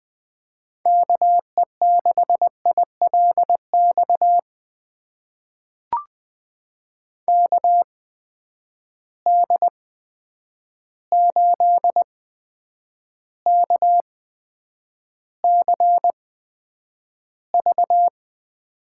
Morse code, heard as a keyed tone, about 20 wpm, spaced as for 9 wpm, 700 Hz. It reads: KE6ILX KD8KCV
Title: USA Call Signs - 6 Characters - 8x Character Spacing - 20wpm